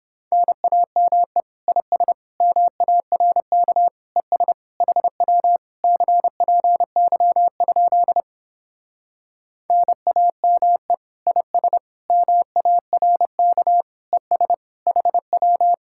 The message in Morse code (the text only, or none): NAME IS MARK ES HW CPY? NAME IS MARK ES HW